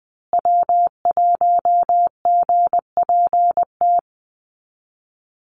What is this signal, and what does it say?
Morse code, heard as a keyed tone, about 20 wpm, 700 Hz. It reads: W1GPT
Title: USA Call Signs Repeated 3x 20wpm